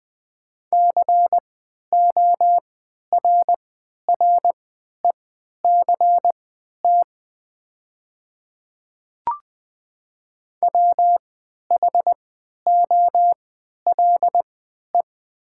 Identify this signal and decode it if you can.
Morse code, heard as a keyed tone, about 20 wpm, spaced as for 11 wpm, 700 Hz. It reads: CORRECT WHOLE
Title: 3x Character Spacing - Top 500 Words - 20wpm